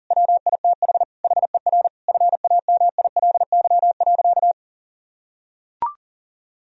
Morse code, heard as a keyed tone, about 40 wpm, 700 Hz. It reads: WITH HER FAMILY.
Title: Sentences from Top 1000 Words - Review Part 4 - 40wpm